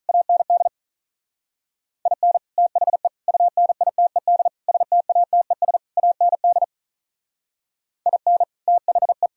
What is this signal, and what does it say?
Morse code, heard as a keyed tone, about 43 wpm, 700 Hz. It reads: AND IN THE UNITED STATES AND IN THE
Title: Speed-Racing - Sets of 5 Words 35wpm